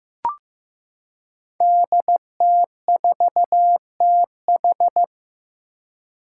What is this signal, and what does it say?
Morse code, heard as a keyed tone, about 15 wpm, 700 Hz. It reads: DT4TH
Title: Five Challenging Letter-Number Combinations 15wpm